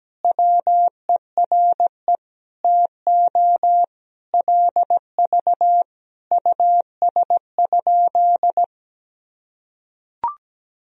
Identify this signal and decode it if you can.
Morse code, heard as a keyed tone, about 17 wpm, 700 Hz. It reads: WERE TO LV US?
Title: Sentences from Top 400 Words and Abbreviations 17wpm